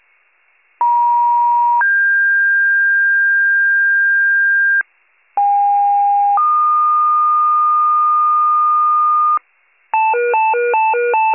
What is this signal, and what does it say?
Two-tone page, heard as a one-second tone, then a three-second tone, about 950 then 1600 Hz, about 810 then 1220 Hz, with noise. It is heard from 0.8 to 9.4 s.